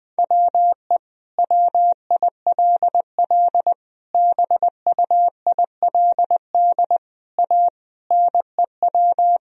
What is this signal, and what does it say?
Morse code, heard as a keyed tone, about 20 wpm, 700 Hz. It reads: WE WILL BUILD A NEW
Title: Sentences from Top 1000 Words - Review Part 3 - 20wpm